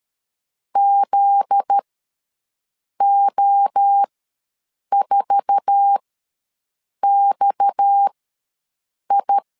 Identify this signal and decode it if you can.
Morse code, heard as a keyed tone, about 13 wpm, spaced as for 6 wpm, 800 Hz. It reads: ZO4XI
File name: LEZIONI / 8 YJ45 / 8 YJ4 ws 12wpm 3249